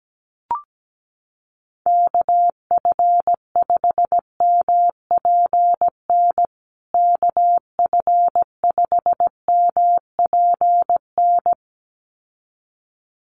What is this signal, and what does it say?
Morse code, heard as a keyed tone, about 17 wpm, 700 Hz. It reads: KF5MPN KF5MPN
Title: USA Call Signs Repeated 2x 17wpm